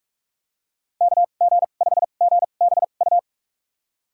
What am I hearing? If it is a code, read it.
Morse code, heard as a keyed tone, about 43 wpm, spaced as for 32 wpm, 700 Hz. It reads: KG5GBU
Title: USA Call Signs - 2x Character Spacing - 45wpm